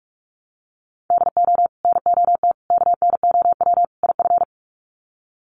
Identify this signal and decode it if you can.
Morse code, heard as a keyed tone, about 44 wpm, 700 Hz. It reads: DO NOT KNOW IF
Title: Sets of 4 Words 45wpm